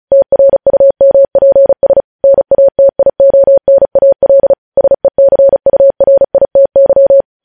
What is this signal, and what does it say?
Morse code, heard as a keyed tone, about 36 wpm, 550 Hz. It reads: TRUMPS NATIONAL SECURITY
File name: podcast / News headlines-1x-35x35